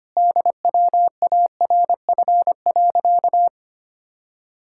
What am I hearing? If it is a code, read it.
Morse code, heard as a keyed tone, about 25 wpm, 700 Hz. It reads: DWARF.
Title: Pangram Challenge v2 - 25wpm